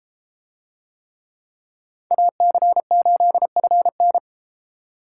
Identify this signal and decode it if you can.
Morse code, heard as a keyed tone, about 32 wpm, 700 Hz. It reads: AC8FN